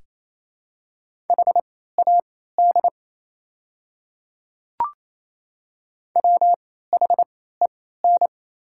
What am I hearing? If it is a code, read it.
Morse code, heard as a keyed tone, about 28 wpm, spaced as for 16 wpm, 690 Hz. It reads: HAD WHEN
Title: 3x Character Spacing - Top 100 Words - 28wpm